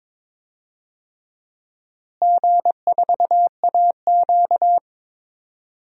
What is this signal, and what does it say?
Morse code, heard as a keyed tone, about 22 wpm, 700 Hz. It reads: G4AQ
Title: DXCC-100 Call Signs - 4 Characters - Repeated 2x - 22wpm